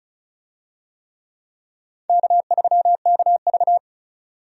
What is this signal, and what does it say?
Morse code, heard as a keyed tone, about 34 wpm, 700 Hz. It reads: K3KV
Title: USA Call Signs - 4 Characters - 35wpm